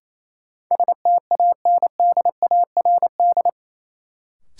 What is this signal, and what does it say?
Morse code, heard as a keyed tone, about 28 wpm, 700 Hz. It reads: STANDARD
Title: Sentences from Top 1200 Words - New Word Drill Alt - 28wpm